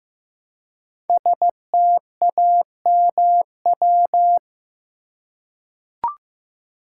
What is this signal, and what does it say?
Morse code, heard as a keyed tone, about 15 wpm, 700 Hz. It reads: STAMW
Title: Code Groups - 5 Letters - 15wpm